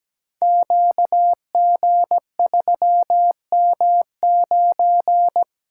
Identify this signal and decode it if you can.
Morse code, heard as a keyed tone, about 17 wpm, 700 Hz. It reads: QG3M9
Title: Code Groups - 5 Letter:Numbers - 17wpm